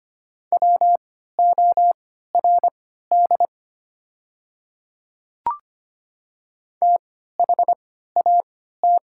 Morse code, heard as a keyed tone, about 25 wpm, spaced as for 14 wpm, 700 Hz. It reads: WORD THAT
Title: 3x Character Spacing - Top 100 Words - 25wpm